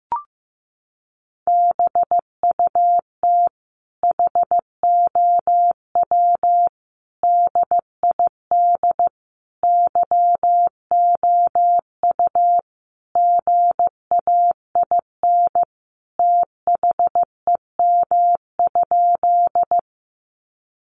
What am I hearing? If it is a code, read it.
Morse code, heard as a keyed tone, about 15 wpm, 700 Hz. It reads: BUT HOW DID YOU GAIN THEM?
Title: Sentences from Top 1000 Words 15wpm